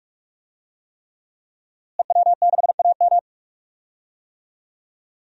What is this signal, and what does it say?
Morse code, heard as a keyed tone, about 46 wpm, 700 Hz. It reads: EW6AM
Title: DXCC-100 Call Signs - 5 Characters - Repeated 3x - 45wpm